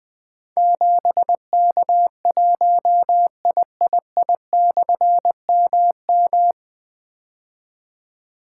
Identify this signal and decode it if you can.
Morse code, heard as a keyed tone, about 20 wpm, 700 Hz. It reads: 7K1III/MM
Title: Difficult Call Signs 20wpm